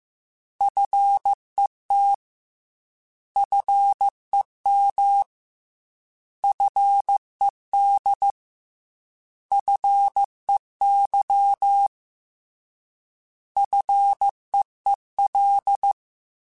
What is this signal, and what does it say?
Morse code, heard as a keyed tone, about 15 wpm, 800 Hz. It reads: FET FEM FED FEY FEEL
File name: Alphabetical / Alphabetical mp3s / Fs @13